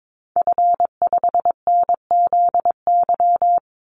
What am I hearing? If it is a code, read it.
Morse code, heard as a keyed tone, about 22 wpm, 700 Hz. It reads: F5NZY
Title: Call Signs Repeated 2x - 22wpm